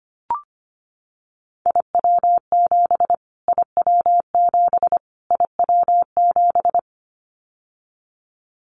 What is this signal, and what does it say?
Morse code, heard as a keyed tone, about 25 wpm, 700 Hz. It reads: IW7 IW7 IW7
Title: Callsign Prefixes for Top 25 DXCC - Repeated 3x - 25wpm